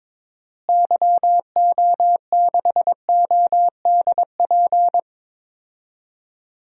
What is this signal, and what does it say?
Morse code, heard as a keyed tone, about 21 wpm, 700 Hz. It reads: YO6ODP
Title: DXCC-100 Call Signs - 6 Characters - Repeated 2x - 22wpm